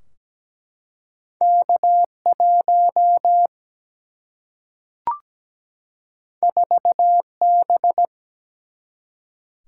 Morse code, heard as a keyed tone, about 17 wpm, 700 Hz. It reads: K 1 4 B
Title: Single-Letter Single-Number 17wpm